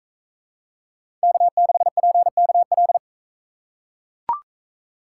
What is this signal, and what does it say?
Morse code, heard as a keyed tone, about 42 wpm, 700 Hz. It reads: K6PKL